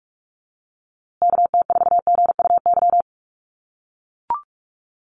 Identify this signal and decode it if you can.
Morse code, heard as a keyed tone, about 44 wpm, 700 Hz. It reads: KT4GUY
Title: USA Call Signs Repeated 2x 45wpm